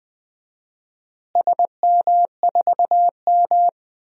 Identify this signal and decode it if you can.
Morse code, heard as a keyed tone, about 20 wpm, 700 Hz. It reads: SM4M
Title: DXCC-100 Call Signs - 4 Characters - Repeated 2x - 20wpm